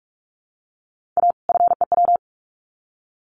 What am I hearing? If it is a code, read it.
Morse code, heard as a keyed tone, about 44 wpm, 700 Hz. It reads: A FEW
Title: Sets of 2 Words 45wpm